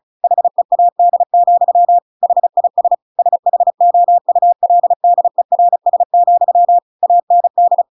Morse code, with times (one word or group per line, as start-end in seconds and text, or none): 0.2-2.0 HEAD,
2.2-2.9 HIS
3.2-6.8 SHOULDERS,
7.0-7.8 AND